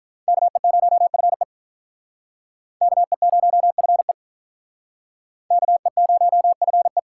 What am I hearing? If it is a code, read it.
Morse code, heard as a keyed tone, about 42 wpm, 700 Hz. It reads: KE0FE KE0FE KE0FE
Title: Speed Racing - USA 2x2 Call Signs 35wpm